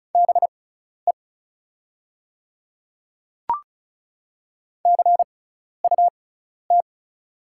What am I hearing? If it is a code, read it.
Morse code, heard as a keyed tone, about 34 wpm, spaced as for 12 wpm, 690 Hz. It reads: BE CUT
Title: 6x Character Spacing - Top 500 Words - 35wpm